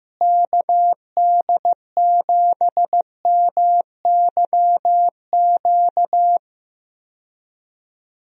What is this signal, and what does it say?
Morse code, heard as a keyed tone, about 15 wpm, 700 Hz. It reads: KD7MYQ
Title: USA Call Signs - 6 Characters - Repeated 3x - 15wpm